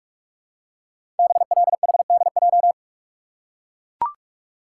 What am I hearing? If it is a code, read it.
Morse code, heard as a keyed tone, about 44 wpm, 700 Hz. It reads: BLHDJ
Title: Code Groups - 5 Letters - 45wpm